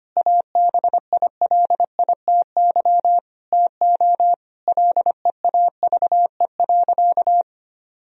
Morse code, heard as a keyed tone, about 25 wpm, 700 Hz. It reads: ABILITY TO LEAVE.
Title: Sentences from Top 1000 Words - Review Part 3 - 25wpm